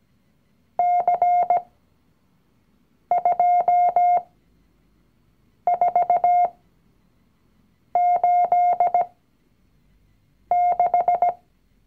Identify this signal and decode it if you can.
Morse code, heard as a keyed tone, about 17 wpm, spaced as for 5 wpm, 710 Hz. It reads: C2486
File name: CW-Online-Kurs / Anfänger / Lektion 019